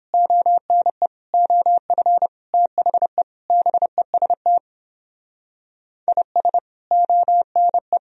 Morse code, heard as a keyed tone, about 29 wpm, 700 Hz. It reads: ONE OF THE BEST IS ONE